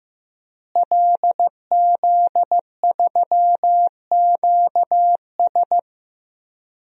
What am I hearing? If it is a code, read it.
Morse code, heard as a keyed tone, about 15 wpm, 700 Hz. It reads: LZ3QS
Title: DXCC-100 Call Signs - 5 Characters - Repeated 3x - 15wpm